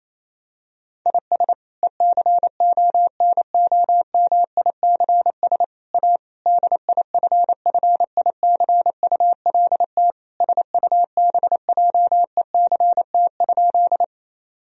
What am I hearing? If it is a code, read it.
Morse code, heard as a keyed tone, about 28 wpm, 700 Hz. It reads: IS ECONOMICS A DIFFICULT SUBJECT?
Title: Sentences from Top 1000 Words - Review Part 3 - 28wpm